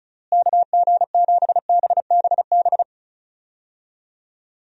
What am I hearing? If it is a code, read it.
Morse code, heard as a keyed tone, about 36 wpm, 700 Hz. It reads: KG7BBB
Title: USA Call Signs - 6 Characters - Repeated 2x - 35wpm